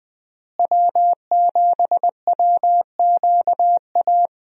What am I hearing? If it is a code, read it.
Morse code, heard as a keyed tone, about 20 wpm, 700 Hz. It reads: W7WQA
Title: USA Call Signs - 5 Characters - Repeated 3x - 20wpm